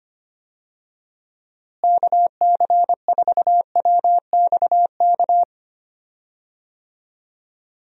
Morse code, heard as a keyed tone, about 25 wpm, 700 Hz. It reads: KC4WXK
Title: USA Call Signs - 6 Characters - Repeated 3x - 25wpm